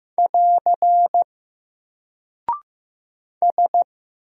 Morse code, heard as a keyed tone, about 15 wpm, 700 Hz. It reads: <AR> S
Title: Single Character (Letter:Number:Punctuation:Prosign) 15wpm